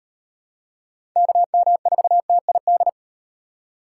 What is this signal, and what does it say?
Morse code, heard as a keyed tone, about 35 wpm, 700 Hz. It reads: KM4TID